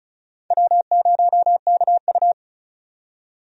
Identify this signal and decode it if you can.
Morse code, heard as a keyed tone, about 34 wpm, 700 Hz. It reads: W0KU
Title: USA Call Signs - 4 Characters - 35wpm